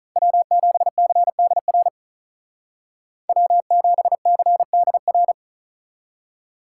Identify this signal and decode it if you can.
Morse code, heard as a keyed tone, about 46 wpm, 700 Hz. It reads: W7CDR W7CDR